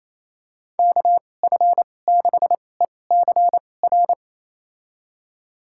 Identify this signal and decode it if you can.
Morse code, heard as a keyed tone, about 28 wpm, spaced as for 20 wpm, 700 Hz. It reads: KF6ECR
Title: USA Call Signs - 6 Characters - 2x Character Spacing - 28wpm